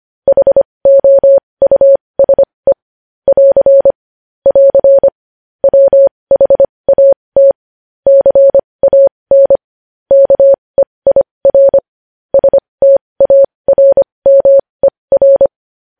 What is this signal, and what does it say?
Morse code, heard as a keyed tone, about 25 wpm, spaced as for 20 wpm, 550 Hz. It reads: HOUSE <AR> <AR> WHAT CAN KEIR STARMER
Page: news headlines in morse code at 20 wpm